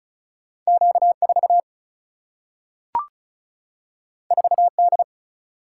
Morse code, heard as a keyed tone, about 36 wpm, 700 Hz. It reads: Q 4 4 D
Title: Single-Letter Single-Number 35wpm